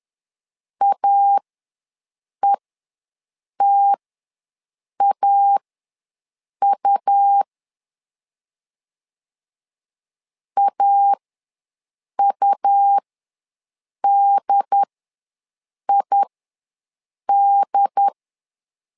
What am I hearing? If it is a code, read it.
Morse code, heard as a keyed tone, about 10 wpm, spaced as for 6 wpm, 800 Hz. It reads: AETAU AUDID
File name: LEZIONI / 2 UHDN / 2 UHD ws 10wpm 2058